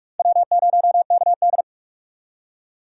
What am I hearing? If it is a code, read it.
Morse code, heard as a keyed tone, about 47 wpm, 700 Hz. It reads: W0KD
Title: USA Call Signs - 4 Characters - 2x Character Spacing - 45wpm